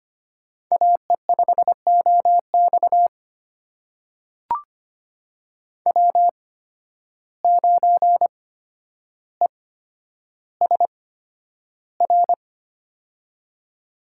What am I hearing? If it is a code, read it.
Morse code, heard as a keyed tone, about 25 wpm, spaced as for 11 wpm, 700 Hz. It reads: AE5OX W9ESR
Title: USA Call Signs - 5 Characters - 8x Character Spacing - 25wpm